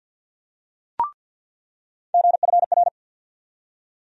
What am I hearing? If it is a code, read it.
Morse code, heard as a keyed tone, about 46 wpm, 700 Hz. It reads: GFR
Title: USA Call Sign Postfixes - 3 Letters 50wpm